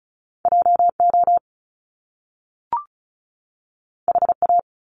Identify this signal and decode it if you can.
Morse code, heard as a keyed tone, about 36 wpm, 700 Hz. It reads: JO HA
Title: Top Two Letter Combinations in Words 35wpm